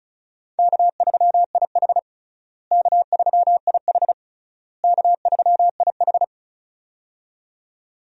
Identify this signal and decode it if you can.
Morse code, heard as a keyed tone, about 34 wpm, 700 Hz. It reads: K3IH K3IH K3IH
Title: USA Call Signs - 4 Characters - Repeated 3x - 35wpm